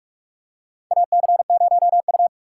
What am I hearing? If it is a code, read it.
Morse code, heard as a keyed tone, about 46 wpm, 700 Hz. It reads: AC0U
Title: USA Call Signs - 2x Character Spacing - 45wpm